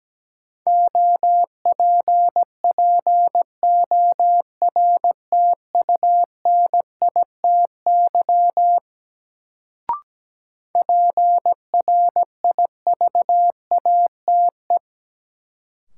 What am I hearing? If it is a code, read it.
Morse code, heard as a keyed tone, about 17 wpm, 700 Hz. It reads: OPPORTUNITY PRIVATE
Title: Sentences from Top 1000 Words - New Word Drill - 17wpm